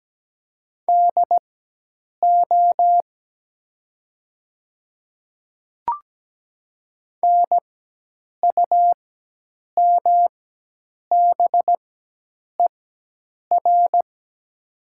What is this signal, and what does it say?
Morse code, heard as a keyed tone, about 17 wpm, spaced as for 8 wpm, 700 Hz. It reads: DO NUMBER
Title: 4x Character Spacing - Top 100 Words - 17wpm